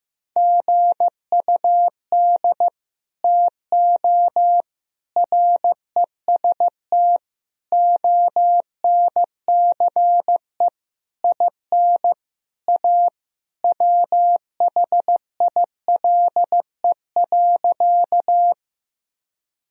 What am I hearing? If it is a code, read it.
Morse code, heard as a keyed tone, about 15 wpm, 700 Hz. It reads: GUD TO REST ONCE IN A WHILE.
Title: Sentences from Top 500 Words and Abbreviations 15wpm